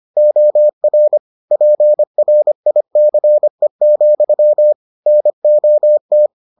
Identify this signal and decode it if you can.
Morse code, heard as a keyed tone, about 25 wpm, 600 Hz. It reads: OR PRICE, NOT